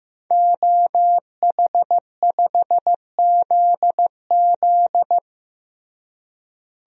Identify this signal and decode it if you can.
Morse code, heard as a keyed tone, about 15 wpm, 700 Hz. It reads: OH5ZZ
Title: DXCC-100 Call Signs - 5 Characters - Repeated 3x - 15wpm